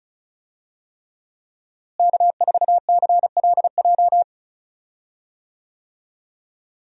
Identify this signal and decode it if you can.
Morse code, heard as a keyed tone, about 34 wpm, 700 Hz. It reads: K4CLJ
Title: USA Call Signs - 5 Characters - Repeated 3x - 35wpm